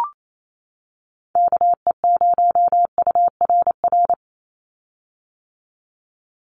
Morse code, heard as a keyed tone, about 28 wpm, 700 Hz. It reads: KE0URR
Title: USA Call Signs 28wpm